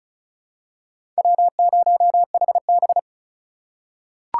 Morse code, heard as a keyed tone, about 36 wpm, 700 Hz. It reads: W0HB
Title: USA Call Signs - 4 Characters - 4x Character Spacing - 35wpm